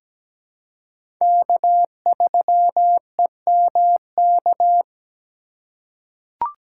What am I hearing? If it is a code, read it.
Morse code, heard as a keyed tone, about 17 wpm, 700 Hz. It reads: K3EMK
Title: USA Call Signs - 5 Characters - 2x Character Spacing - 17wpm